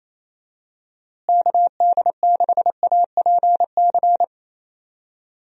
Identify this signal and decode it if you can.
Morse code, heard as a keyed tone, about 28 wpm, 700 Hz. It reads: KD6APC
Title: USA Call Signs - 6 Characters - Repeated 2x - 28wpm